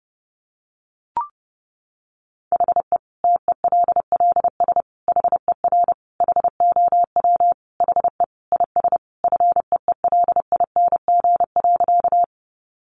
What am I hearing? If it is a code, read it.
Morse code, heard as a keyed tone, about 30 wpm, 700 Hz. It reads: HE TELLS HER HOW HE IS FEELING.
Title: Additional Easy Phrases 30wpm